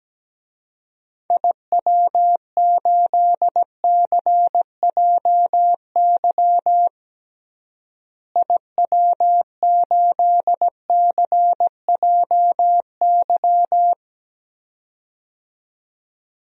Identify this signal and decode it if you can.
Morse code, heard as a keyed tone, about 17 wpm, 700 Hz. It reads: IW8CJY IW8CJY
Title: DXCC-100 Call Signs - 6 Characters - Repeated 2x - 17wpm